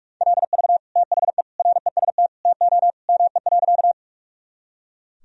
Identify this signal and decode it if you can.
Morse code, heard as a keyed tone, about 41 wpm, 700 Hz. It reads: LV THE REST TO ME.
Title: Sentences from Top 500 Words and Abbreviations 45wpm